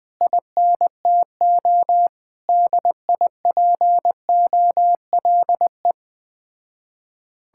Morse code, heard as a keyed tone, about 21 wpm, 700 Hz. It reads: INTO DIPOLE